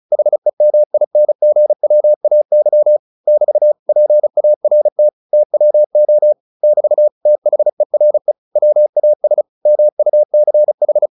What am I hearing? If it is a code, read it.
Morse code, heard as a keyed tone, about 34 wpm, 600 Hz. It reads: HEMINGWAY <BT> PART TWO <BT> THERE WAS MUCH
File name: the old man and the sea 2 35wpm